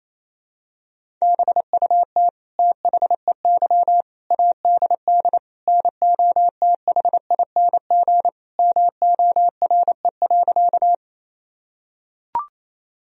Morse code, heard as a keyed tone, about 28 wpm, 700 Hz. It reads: BUT THEY ADD NOTHING MORE.
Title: Sentences from Top 1000 Words - Review Part 1 - 28wpm